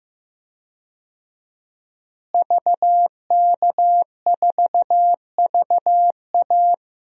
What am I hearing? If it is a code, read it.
Morse code, heard as a keyed tone, about 15 wpm, 700 Hz. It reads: VK4VA
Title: DXCC-100 Call Signs - 5 Characters - Repeated 3x - 15wpm